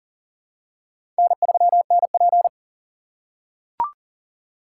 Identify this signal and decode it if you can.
Morse code, heard as a keyed tone, about 40 wpm, 700 Hz. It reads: N3NP
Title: USA Call Signs - 4 Characters - 4x Character Spacing - 40wpm